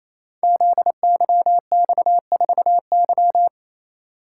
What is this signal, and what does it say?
Morse code, heard as a keyed tone, about 29 wpm, 700 Hz. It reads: ZYX4Y
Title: Code Groups - 5 Letter:Numbers - 28wpm